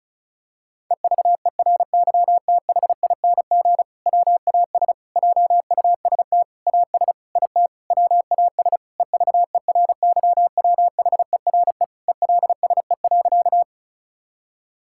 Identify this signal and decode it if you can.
Morse code, heard as a keyed tone, about 36 wpm, 700 Hz. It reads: EVERYTHING WAS JUST AS IT WAS EVERYWHERE ELSE.
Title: Sentences from Top 1600 Words 35wpm